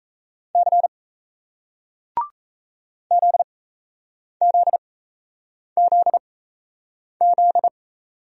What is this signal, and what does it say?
Morse code, heard as a keyed tone, about 38 wpm, spaced as for 8 wpm, 700 Hz. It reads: C ZZZZ